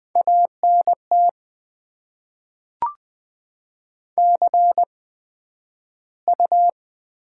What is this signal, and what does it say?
Morse code, heard as a keyed tone, about 20 wpm, spaced as for 11 wpm, 700 Hz. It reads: ANT CU
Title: Top 20 QSO Elements - 8x Character Spacing - 20wpm